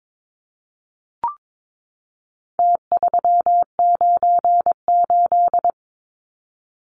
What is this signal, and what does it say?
Morse code, heard as a keyed tone, about 22 wpm, 700 Hz. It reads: T398
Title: Numbers - Contest Style - 4-Digit 22wpm